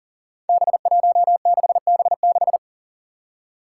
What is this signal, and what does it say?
Morse code, heard as a keyed tone, about 40 wpm, 700 Hz. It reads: B16B6
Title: Five Challenging Letter-Number Combinations 40wpm